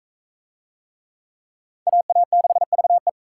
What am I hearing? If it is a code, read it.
Morse code, heard as a keyed tone, about 41 wpm, 700 Hz. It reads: AA6VE